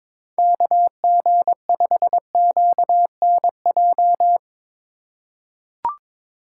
Morse code, heard as a keyed tone, about 21 wpm, 700 Hz. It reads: KG5QNJ
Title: USA Call Signs - 6 Characters - 2x Character Spacing - 22wpm